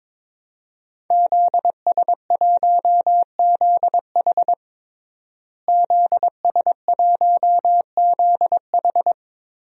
Morse code, heard as a keyed tone, about 22 wpm, 700 Hz. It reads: ZS1ZH ZS1ZH
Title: DXCC-100 Call Signs - 5 Characters - Repeated 2x - 22wpm